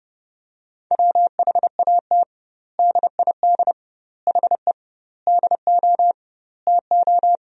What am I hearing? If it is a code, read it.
Morse code, heard as a keyed tone, about 30 wpm, 700 Hz. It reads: WHAT DID HE DO TO
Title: 2x Word-Spacing - Sentences from Top 100 Words - 30wpm